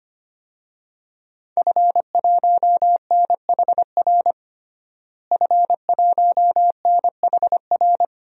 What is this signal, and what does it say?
Morse code, heard as a keyed tone, about 25 wpm, 700 Hz. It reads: F1NHR F1NHR
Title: DXCC-100 Call Signs - 5 Characters - Repeated 2x - 25wpm